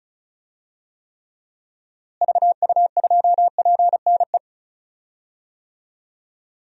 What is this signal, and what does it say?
Morse code, heard as a keyed tone, about 34 wpm, 700 Hz. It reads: VU2PNE
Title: DXCC-100 Call Signs - 6 Characters - Repeated 2x - 35wpm